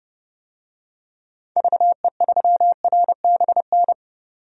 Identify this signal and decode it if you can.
Morse code, heard as a keyed tone, about 30 wpm, 700 Hz. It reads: VE3RBN
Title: DXCC-100 Call Signs - 6 Characters - 30wpm